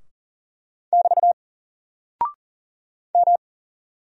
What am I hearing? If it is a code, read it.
Morse code, heard as a keyed tone, about 39 wpm, 700 Hz. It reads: <BT> M
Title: Single Character (Letter:Number:Punctuation:Prosign) Alt 40wpm